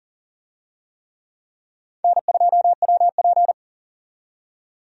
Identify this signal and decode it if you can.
Morse code, heard as a keyed tone, about 40 wpm, 700 Hz. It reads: N2WP
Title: USA Call Signs - 4 Characters - 40wpm